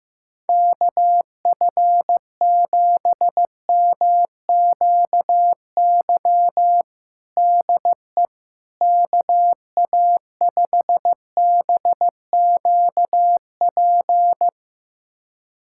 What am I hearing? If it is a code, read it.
Morse code, heard as a keyed tone, about 15 wpm, 700 Hz. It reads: KF7MQY DE KA5BQP
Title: USA Callsign DE Callsign - 6 Characters - 15wpm